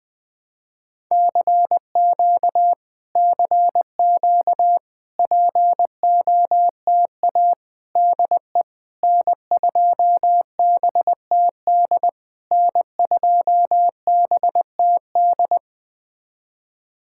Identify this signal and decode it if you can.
Morse code, heard as a keyed tone, about 20 wpm, 700 Hz. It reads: CQ CQ POTA DE N2BTD N2BTD
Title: POTA QSO Practice - 20wpm